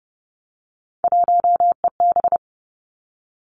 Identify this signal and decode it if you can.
Morse code, heard as a keyed tone, about 30 wpm, 700 Hz. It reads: 1EB
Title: Three Challenging Letter-Number Combinations 30wpm